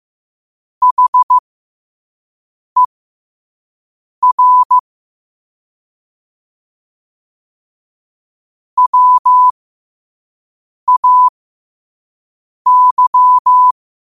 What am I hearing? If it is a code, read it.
Morse code, heard as a keyed tone, about 15 wpm, spaced as for 5 wpm, 1000 Hz. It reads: HER WAY
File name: qotdfiles / 5wpm